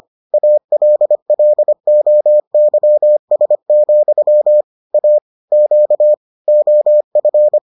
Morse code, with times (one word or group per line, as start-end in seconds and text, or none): 0.3-4.6 ALLOYS,
4.9-5.2 A
5.5-6.1 Q
6.5-7.6 OF